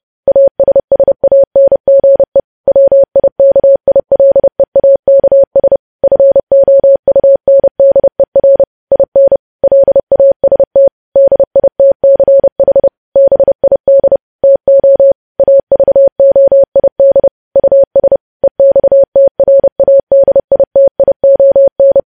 Morse code, heard as a keyed tone, about 30 wpm, 550 Hz. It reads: ASSANGE WIKILEAKS FOUNDER IN LAST DITCH BID TO AVOID US EXTRADITION
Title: News Headlines at 30